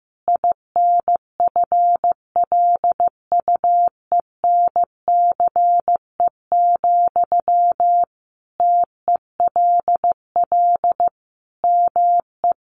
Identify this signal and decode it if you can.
Morse code, heard as a keyed tone, about 15 wpm, 700 Hz. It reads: INFLUENCE, TELL ME
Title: Sentences from Top 1100 Words 15wpm